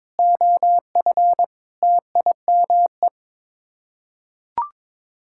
Morse code, heard as a keyed tone, about 21 wpm, 700 Hz. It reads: OF TIME
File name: practice / 22wpm / Sets of 7 Words - 22wpm